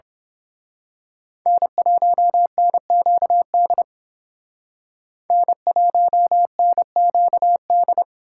Morse code, heard as a keyed tone, about 26 wpm, 700 Hz. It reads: N1NQD N1NQD
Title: Speed Racing - USA 1x3 Call Signs 22wpm